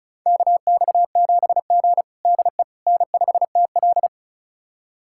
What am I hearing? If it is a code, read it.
Morse code, heard as a keyed tone, about 36 wpm, 700 Hz. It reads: KX7G DE N5TL
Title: USA Callsign DE Callsign - 4 Characters - 35wpm